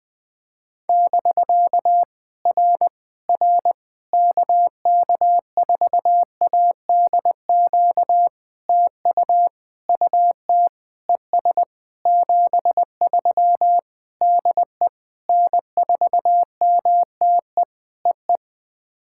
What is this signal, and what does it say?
Morse code, heard as a keyed tone, about 20 wpm, 700 Hz. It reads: <BK> R R KK4ADQ TU UT ES 73 DE N4MTE EE
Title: POTA QSO Practice - 20wpm